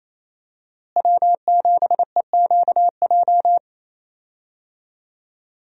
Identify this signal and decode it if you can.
Morse code, heard as a keyed tone, about 28 wpm, 700 Hz. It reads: W7EQJ
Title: USA Call Signs - 5 Characters - Repeated 2x - 28wpm